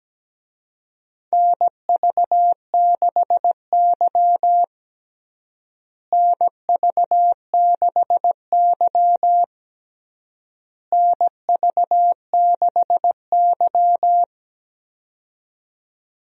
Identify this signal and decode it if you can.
Morse code, heard as a keyed tone, about 17 wpm, 700 Hz. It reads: NV6Y NV6Y NV6Y
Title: USA Call Signs - 4 Characters - Repeated 3x - 17wpm